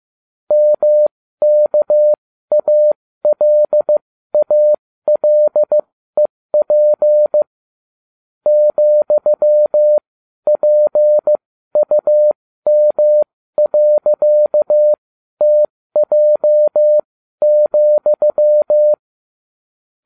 Morse code, heard as a keyed tone, about 15 wpm, spaced as for 12 wpm, 600 Hz. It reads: MKALALEP ,PUM.TJ,